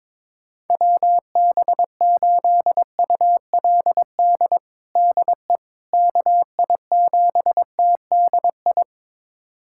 Morse code, heard as a keyed tone, about 21 wpm, 700 Hz. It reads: WB8ULD DE KI7TDI
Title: USA Callsign DE Callsign - 6 Characters - 22wpm